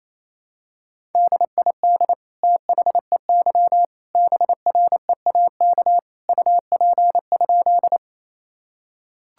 Morse code, heard as a keyed tone, about 27 wpm, 700 Hz. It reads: DID THEY BREAK UP?
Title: Sentences from Top 1000 Words - Review Part 6 - 28wpm